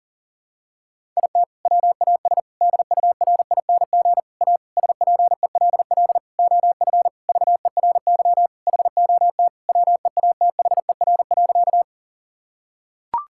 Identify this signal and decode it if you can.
Morse code, heard as a keyed tone, about 40 wpm, 700 Hz. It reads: IT WAS DURING A SPELL OF VERY HOT WEATHER.
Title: Sentences from Top 1000 Words - Review Part 5 - 40wpm